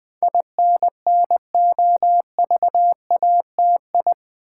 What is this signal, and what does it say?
Morse code, heard as a keyed tone, about 20 wpm, 700 Hz. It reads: INNOVATI
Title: Build A Word - 4 Letters At A Time - 10-Letter Words - 20wpm